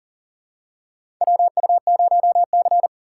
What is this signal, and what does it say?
Morse code, heard as a keyed tone, about 40 wpm, 700 Hz. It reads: WU0C
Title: USA Call Signs - 8x Character Spacing - 40wpm